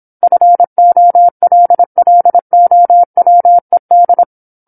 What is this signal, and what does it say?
Morse code, heard as a keyed tone, about 27 wpm, 700 Hz. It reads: FOLLOWED
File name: 08-Chapter-08 26 WPM